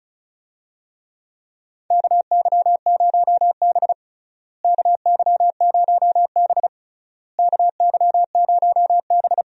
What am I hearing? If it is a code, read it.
Morse code, heard as a keyed tone, about 36 wpm, 700 Hz. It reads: KY0B KY0B KY0B
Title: USA Call Signs - 4 Characters - Repeated 3x - 35wpm